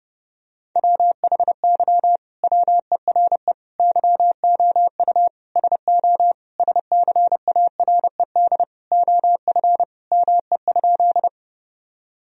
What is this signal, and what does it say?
Morse code, heard as a keyed tone, about 30 wpm, 700 Hz. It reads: WHY WERE YOU SO SCARED OF ME?
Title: Sentences from Top 1000 Words - Review Part 4 - 30wpm